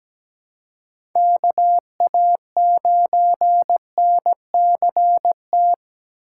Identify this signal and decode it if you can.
Morse code, heard as a keyed tone, about 17 wpm, 700 Hz. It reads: KA9NCT
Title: USA Call Signs - 6 Characters - 3x Character Spacing - 17wpm